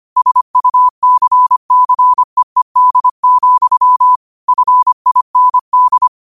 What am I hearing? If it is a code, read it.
Morse code, heard as a keyed tone, about 25 wpm, 1000 Hz. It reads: SUCCEED, FIND
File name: qotdfiles / 25wpm